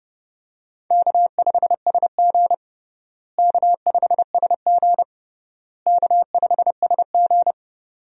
Morse code, heard as a keyed tone, about 30 wpm, 700 Hz. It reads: K5SG K5SG K5SG
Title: USA Call Signs - 4 Characters - Repeated 3x - 30wpm